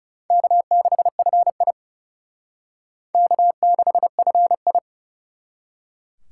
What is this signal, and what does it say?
Morse code, heard as a keyed tone, about 40 wpm, 700 Hz. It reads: K6FI K6FI